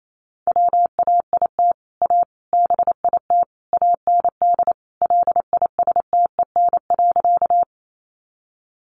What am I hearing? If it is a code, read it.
Morse code, heard as a keyed tone, about 28 wpm, 700 Hz. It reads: WAIT A BIT AND LISTEN.
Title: Sentences from Top 900 Words 28wpm